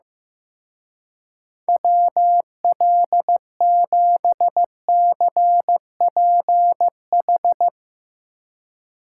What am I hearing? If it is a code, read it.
Morse code, heard as a keyed tone, about 16 wpm, 700 Hz. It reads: WL7CPH
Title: Speed Racing - USA 2x3 Call Signs 15wpm